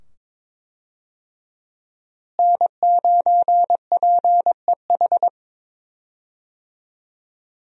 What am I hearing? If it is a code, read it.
Morse code, heard as a keyed tone, about 22 wpm, 700 Hz. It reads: N9PEH
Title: USA Call Signs - 5 Characters - Repeated 2x - 22wpm